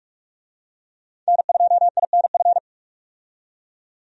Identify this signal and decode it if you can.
Morse code, heard as a keyed tone, about 46 wpm, 700 Hz. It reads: N2INF
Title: USA Call Signs - 5 Characters - 45wpm